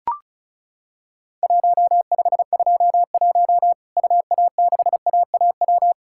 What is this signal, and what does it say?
Morse code, heard as a keyed tone, about 34 wpm, 700 Hz. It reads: 1521 UA6AAW
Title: Worked All Europe DX Contest - QTC Exchange - 35wpm